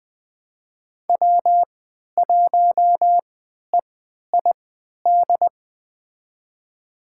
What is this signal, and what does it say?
Morse code, heard as a keyed tone, about 20 wpm, spaced as for 11 wpm, 700 Hz. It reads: W1EID
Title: USA Call Signs - 5 Characters - 3x Character Spacing - 20wpm